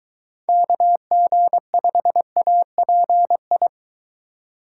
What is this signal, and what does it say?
Morse code, heard as a keyed tone, about 23 wpm, 700 Hz. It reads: KG5API